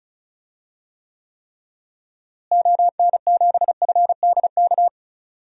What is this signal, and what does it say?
Morse code, heard as a keyed tone, about 36 wpm, 700 Hz. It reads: ON7FDK
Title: DXCC-100 Call Signs - 6 Characters - Repeated 3x - 35wpm